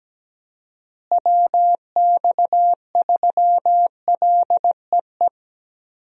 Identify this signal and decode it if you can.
Morse code, heard as a keyed tone, about 17 wpm, 700 Hz. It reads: WX3LEE
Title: USA Call Signs - 6 Characters - Repeated 2x - 17wpm